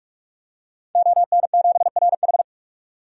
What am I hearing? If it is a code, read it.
Morse code, heard as a keyed tone, about 45 wpm, 700 Hz. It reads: ON7RH
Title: DXCC-100 Call Signs - 5 Characters - 45wpm